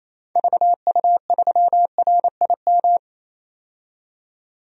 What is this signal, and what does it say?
Morse code, heard as a keyed tone, about 28 wpm, 700 Hz. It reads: VU3RIM
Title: DXCC-100 Call Signs - 6 Characters - 28wpm